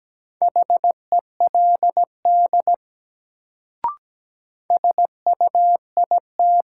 Morse code, heard as a keyed tone, about 17 wpm, 700 Hz. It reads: HELD SUIT
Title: Sentences from Top 1600 Words - New Word Drill Alt - 17wpm